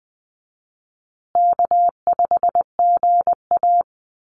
Morse code, heard as a keyed tone, about 20 wpm, 700 Hz. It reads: K5GA
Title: Easy Call Signs Repeated 2x - 20wpm